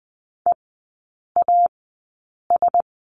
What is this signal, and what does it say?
Morse code, heard as a keyed tone, about 20 wpm, spaced as for 8 wpm, 700 Hz. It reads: EAS
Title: 093 Lesson 16e - Letter c - Three Characters 20wpm